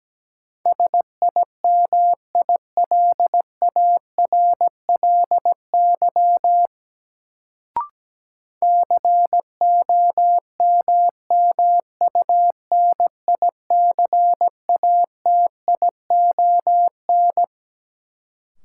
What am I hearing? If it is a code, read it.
Morse code, heard as a keyed tone, about 17 wpm, 700 Hz. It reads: SIMILARLY COMMUNICATION
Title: Sentences from Top 1400 Words - New Word Drill Alt - 17wpm